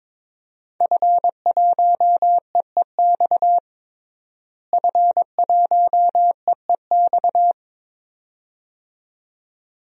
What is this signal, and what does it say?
Morse code, heard as a keyed tone, about 22 wpm, 700 Hz. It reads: F1EEX F1EEX
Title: DXCC-100 Call Signs - 5 Characters - Repeated 2x - 22wpm